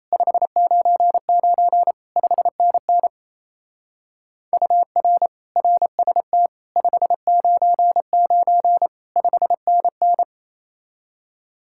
Morse code, heard as a keyed tone, about 34 wpm, 700 Hz. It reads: 599 5NN UR RST 599 5NN
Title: Speed-Racing Common QSO Phrases - 28wpm